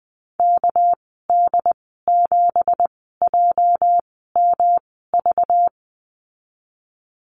Morse code, heard as a keyed tone, about 20 wpm, spaced as for 15 wpm, 700 Hz. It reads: KD7JMV
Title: 219 Lesson 35g - Number 0 - Call Signs 20wpm